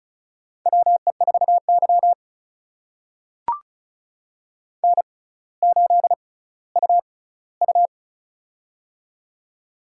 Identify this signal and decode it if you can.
Morse code, heard as a keyed tone, about 36 wpm, spaced as for 18 wpm, 700 Hz. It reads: WE4Y N8UU
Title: USA Call Signs - 4 Characters - 6x Character Spacing - 35wpm